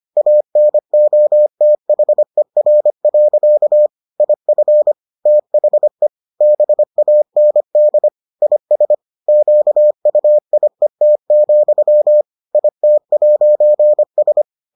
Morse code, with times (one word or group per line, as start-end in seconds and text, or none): 0.2-3.9 ANOTHER.
4.2-4.9 IF
5.3-6.1 THE
6.4-8.1 BAND
8.4-8.9 IS
9.3-12.2 QUIET,
12.5-14.4 IT'S